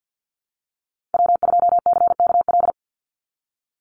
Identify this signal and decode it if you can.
Morse code, heard as a keyed tone, about 48 wpm, 700 Hz. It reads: W2CKL
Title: NA SOTA Activator Callsigns 50wpm